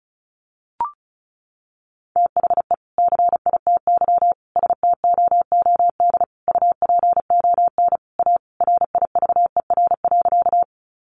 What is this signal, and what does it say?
Morse code, heard as a keyed tone, about 34 wpm, 700 Hz. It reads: THE CITY STOOD UPON A RIVER.
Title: Sentences from Top 1100 Words 35wpm